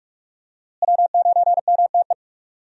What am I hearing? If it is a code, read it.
Morse code, heard as a keyed tone, about 47 wpm, 700 Hz. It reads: W9MTE
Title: USA Call Signs - 5 Characters - Repeated 2x - 45wpm